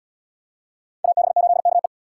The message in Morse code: AB8KE

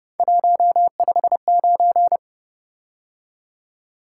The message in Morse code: 159

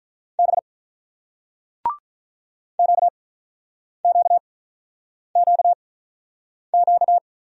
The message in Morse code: B QQQQ

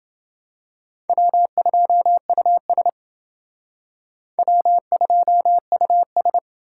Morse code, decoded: W2US W2US

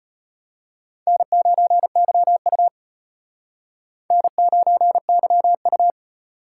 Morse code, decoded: N9YU N9YU

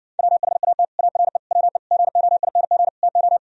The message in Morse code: WHAT ARE WE GOING TO

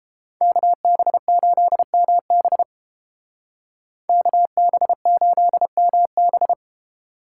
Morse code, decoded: KB8MB KB8MB

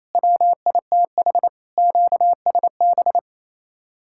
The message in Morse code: WITH QSB

